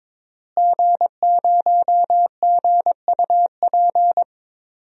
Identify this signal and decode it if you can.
Morse code, heard as a keyed tone, about 22 wpm, 700 Hz. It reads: G0GUP
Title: DXCC-100 Call Signs - 5 Characters - Repeated 3x - 22wpm